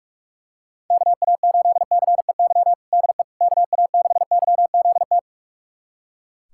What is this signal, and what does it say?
Morse code, heard as a keyed tone, about 43 wpm, 700 Hz. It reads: KA8CEY DE KA6YZT
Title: USA Callsign DE Callsign - 6 Characters - 45wpm